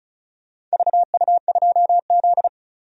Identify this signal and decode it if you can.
Morse code, heard as a keyed tone, about 34 wpm, 700 Hz. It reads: VU2Z
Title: DXCC-100 Call Signs - 4 Characters - 35wpm